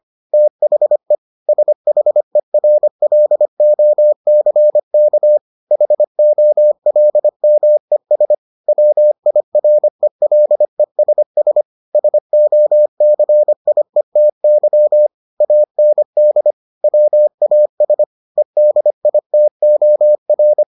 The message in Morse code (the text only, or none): THE SHERLOCK HOLMES WIRELESS SOCIETY AND WAS EDITOR